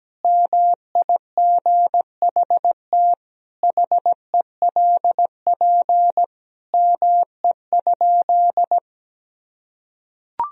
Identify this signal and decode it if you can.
Morse code, heard as a keyed tone, about 17 wpm, 700 Hz. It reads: MIGHT HELP ME?
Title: Sentences from Top 1000 Words - Review Part 3 - 17wpm